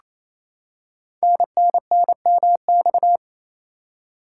Morse code, heard as a keyed tone, about 27 wpm, 700 Hz. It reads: NNNMX
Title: Code Groups - 5 Letters - 28wpm